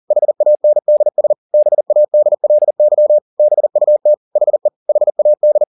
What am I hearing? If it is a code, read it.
Morse code, heard as a keyed tone, about 40 wpm, 600 Hz. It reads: HANDS BADLY BUT HE HAD
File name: the old man and the sea 2 40wpm